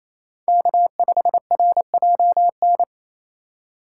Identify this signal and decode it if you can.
Morse code, heard as a keyed tone, about 28 wpm, 700 Hz. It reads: K5RJN
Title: USA Call Signs - 4x Character Spacing - 28wpm